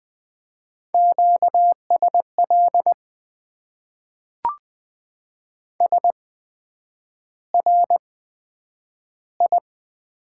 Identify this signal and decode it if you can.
Morse code, heard as a keyed tone, about 20 wpm, spaced as for 9 wpm, 700 Hz. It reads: QSL SRI
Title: Top 90 QSO Elements - 8x Character Spacing - 20wpm